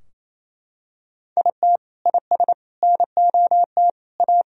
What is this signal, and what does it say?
Morse code, heard as a keyed tone, about 28 wpm, 700 Hz. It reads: IT IS NOT A
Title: Sentences from Top 1000 Words - Review Part 4 - 28wpm